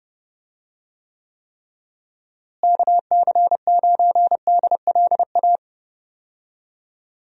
Morse code, heard as a keyed tone, about 30 wpm, 700 Hz. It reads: KC9DLA
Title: USA Call Signs - 6 Characters - Repeated 3x - 30wpm